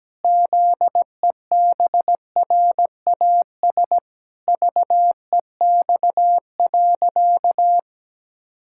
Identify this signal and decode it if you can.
Morse code, heard as a keyed tone, about 17 wpm, 700 Hz. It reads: ZEBRAS VEX.
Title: Pangram Challenge v2 - 17wpm